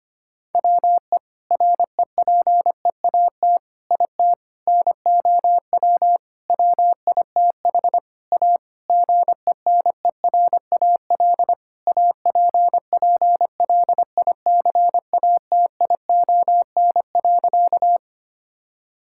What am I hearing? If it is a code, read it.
Morse code, heard as a keyed tone, about 25 wpm, 700 Hz. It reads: WE REPEAT IT NOW WITH A GENERAL APPLICATION.
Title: Sentences from Top 1600 Words 25wpm